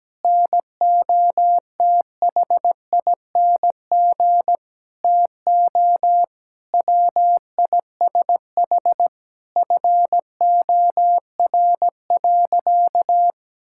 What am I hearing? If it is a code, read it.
Morse code, heard as a keyed tone, about 17 wpm, 700 Hz. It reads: NOTHING TO WISH FOR.
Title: Sentences from Top 1600 Words 17wpm